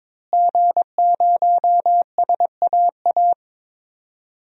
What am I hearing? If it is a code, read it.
Morse code, heard as a keyed tone, about 22 wpm, 700 Hz. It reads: G0SAA
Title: DXCC-100 Call Signs - 5 Characters - Repeated 2x - 22wpm